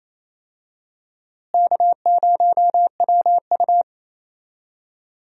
Morse code, heard as a keyed tone, about 28 wpm, 700 Hz. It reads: K0WU
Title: USA Call Signs - 4 Characters - Repeated 2x - 28wpm